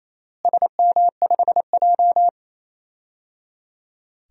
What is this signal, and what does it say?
Morse code, heard as a keyed tone, about 28 wpm, 700 Hz. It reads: SM5J
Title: DXCC-100 Call Signs - 4 Characters - Repeated 3x - 28wpm